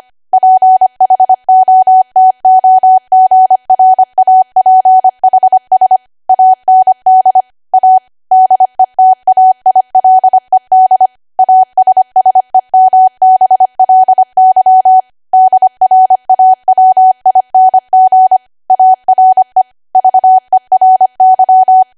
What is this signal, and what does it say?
Morse code, heard as a keyed tone, about 25 wpm, 750 Hz. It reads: PHOTOGRAPHS AND A DETAILED ASSEMBLY DRAWING ARE VERY
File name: Archive / 25 WPM / 25WPM